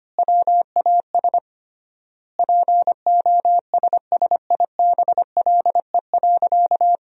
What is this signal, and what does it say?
Morse code, heard as a keyed tone, about 25 wpm, 700 Hz. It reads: WAS POSSIBLE.
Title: 3x Word-Spacing - Sentences from Top 500 Words - 25wpm